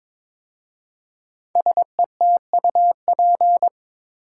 Morse code, heard as a keyed tone, about 22 wpm, 700 Hz. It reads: SETUP
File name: 5 Letter Words - First 3 Letters Given - 22wpm